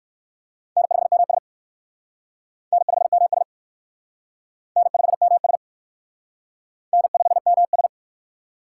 Morse code, heard as a keyed tone, about 54 wpm, 700 Hz. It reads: N5MS N5MS N5MS N5MS
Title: Speed Racing - USA 1x2 Call Signs 45wpm